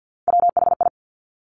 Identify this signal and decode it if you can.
Morse code, heard as a keyed tone, about 50 wpm, 700 Hz. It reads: WHI
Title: Top Three Letter Combinations in Words 50wpm